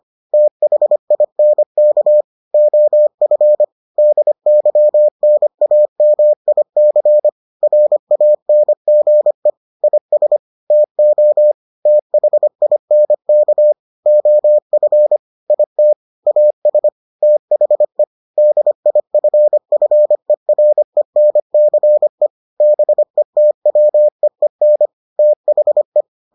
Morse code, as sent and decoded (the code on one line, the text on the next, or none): - .... .. -. -.- / --- ..-. / -.. -.-- -. .- -- .. -.-. / .-. .- -. --. / .. ... / - --- / - .... .. -. -.- / --- ..-. / .. - / .- ... / - .... / -.. .. ..-. ..-. .-. -. -.-. / -... - .-- . -. / - .... .
THINK OF DYNAMIC RANGE IS TO THINK OF IT AS THE DIFFERENCE BETWEEN THE